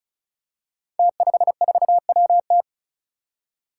Morse code, heard as a keyed tone, about 36 wpm, 700 Hz. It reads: T54WT
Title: Five Challenging Letter-Number Combinations 35wpm